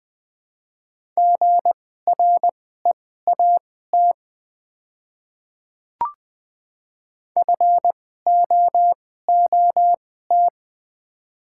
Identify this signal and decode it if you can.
Morse code, heard as a keyed tone, about 20 wpm, spaced as for 14 wpm, 700 Hz. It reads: GREAT FOOT
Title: 2x Character Spacing - Top 500 Words - 20wpm